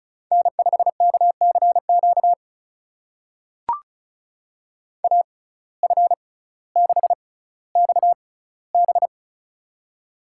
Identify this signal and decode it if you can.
Morse code, heard as a keyed tone, about 36 wpm, spaced as for 18 wpm, 700 Hz. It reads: N5KCQ AF6XB
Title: USA Call Signs - 5 Characters - 6x Character Spacing - 35wpm